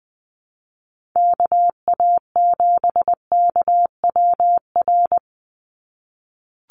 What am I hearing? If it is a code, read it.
Morse code, heard as a keyed tone, about 20 wpm, 700 Hz. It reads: KA7KWR
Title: USA Call Signs 20wpm